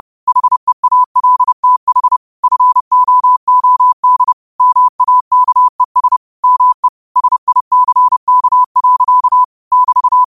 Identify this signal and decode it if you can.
Morse code, heard as a keyed tone, about 30 wpm, 1000 Hz. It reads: HEALTH FOOD MAKES ME SICK. <BT>
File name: qotdfiles / 30wpm